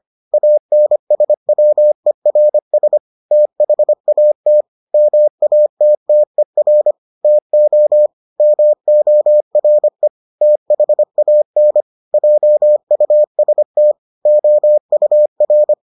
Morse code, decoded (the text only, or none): ANSWERS THAT MATTER TO MORE THAN JUST OUR